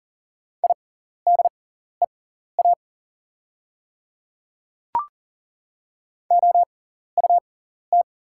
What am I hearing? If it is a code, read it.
Morse code, heard as a keyed tone, about 40 wpm, spaced as for 14 wpm, 690 Hz. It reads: IDEA OUT
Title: 6x Character Spacing - Top 500 Words - 40wpm